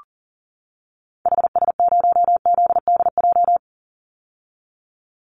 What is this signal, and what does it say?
Morse code, heard as a keyed tone, about 38 wpm, 700 Hz. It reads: HS0ZDJ
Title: Call Signs - 40wpm